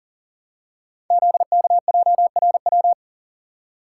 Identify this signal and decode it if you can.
Morse code, heard as a keyed tone, about 42 wpm, 700 Hz. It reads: ZKJRW
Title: Code Groups - 5 Letter:Numbers - 40wpm